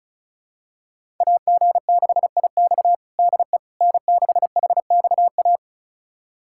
Morse code, heard as a keyed tone, about 36 wpm, 700 Hz. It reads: AG6IX DE N6HXA
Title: USA Callsign DE Callsign - 5 Characters - 35wpm